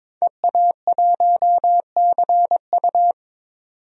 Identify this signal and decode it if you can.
Morse code, heard as a keyed tone, about 22 wpm, 700 Hz. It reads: EA1CU
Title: DXCC-100 Call Signs - 5 Characters - 22wpm